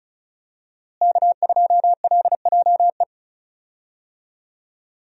Morse code, heard as a keyed tone, about 34 wpm, 700 Hz. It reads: K2LJE
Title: USA Call Signs - 5 Characters - Repeated 3x - 35wpm